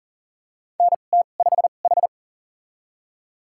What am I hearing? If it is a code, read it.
Morse code, heard as a keyed tone, about 40 wpm, spaced as for 29 wpm, 700 Hz. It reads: NT5H
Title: USA Call Signs - 2x Character Spacing - 40wpm